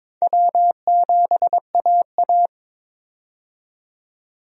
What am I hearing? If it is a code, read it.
Morse code, heard as a keyed tone, about 22 wpm, 700 Hz. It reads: W7AA
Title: USA Call Signs - 4 Characters - 22wpm